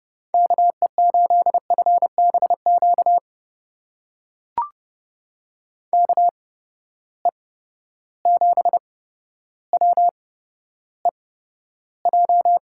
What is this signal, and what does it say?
Morse code, heard as a keyed tone, about 30 wpm, spaced as for 13 wpm, 700 Hz. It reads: KE8FBQ KE7WEJ
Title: USA Call Signs - 6 Characters - 8x Character Spacing - 30wpm